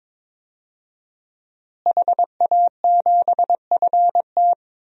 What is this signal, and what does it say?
Morse code, heard as a keyed tone, about 22 wpm, 700 Hz. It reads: HA7FT
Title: DXCC-100 Call Signs - 5 Characters - Repeated 3x - 22wpm